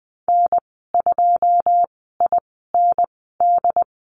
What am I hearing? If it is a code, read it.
Morse code, heard as a keyed tone, about 20 wpm, spaced as for 14 wpm, 700 Hz. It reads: N2IND
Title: 082 Lesson 14g - Number 2 - Call Signs 20wpm